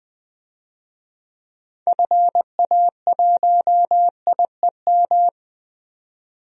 Morse code, heard as a keyed tone, about 20 wpm, 700 Hz. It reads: FA1IEM
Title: DXCC-100 Call Signs - 6 Characters - 20wpm